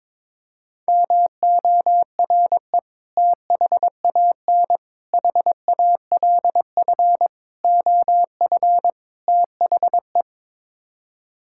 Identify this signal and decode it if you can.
Morse code, heard as a keyed tone, about 22 wpm, 700 Hz. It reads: MORE THAN HALF OF THE
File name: practice / 22wpm / Build A Phrase - 5-Word Phrases - 22wpm